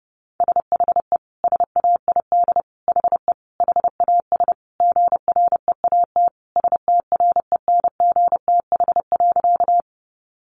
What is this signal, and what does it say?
Morse code, heard as a keyed tone, about 30 wpm, 700 Hz. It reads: SHE SAID HE HAS GREAT STRENGTH.
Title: Sentences from Top 1300 Words 30wpm